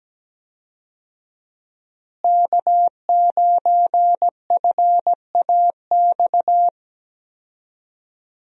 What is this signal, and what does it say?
Morse code, heard as a keyed tone, about 17 wpm, 700 Hz. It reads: K9FAX
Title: USA Call Signs - 5 Characters - Repeated 3x - 17wpm